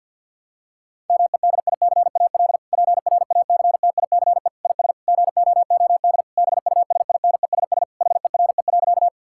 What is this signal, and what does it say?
Morse code, heard as a keyed tone, about 50 wpm, 700 Hz. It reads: MEDICAL PRACTICE IS GOOD BUSINESS HERE.
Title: Sentences from Top 1000 Words - Review Part 5 - 50wpm